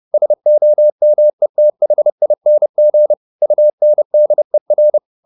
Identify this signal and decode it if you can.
Morse code, heard as a keyed tone, about 30 wpm, 600 Hz. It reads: SOMETHING UNDER
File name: the old man and the sea 2 30wpm